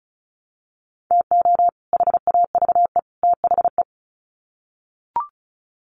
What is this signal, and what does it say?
Morse code, heard as a keyed tone, about 34 wpm, 700 Hz. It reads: TO HAVE THE
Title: Sets of 3 Words 35wpm